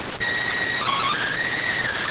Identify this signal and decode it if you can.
SSTV (slow-scan television), PD 290.